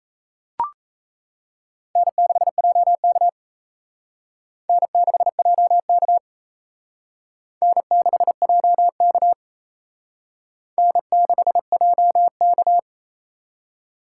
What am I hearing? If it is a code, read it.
Morse code, heard as a keyed tone, about 35 wpm, 700 Hz. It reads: N6JK N6JK N6JK N6JK
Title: Speed Racing - USA 1x2 Call Signs 28wpm